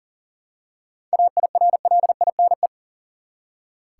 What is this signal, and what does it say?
Morse code, heard as a keyed tone, about 40 wpm, 700 Hz. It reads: AIRLINE
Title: Seven Letter Words - 40wpm